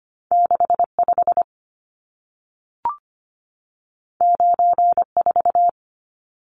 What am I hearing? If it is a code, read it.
Morse code, heard as a keyed tone, about 25 wpm, 700 Hz. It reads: 65 94
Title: Numbers - 2-Digit 25wpm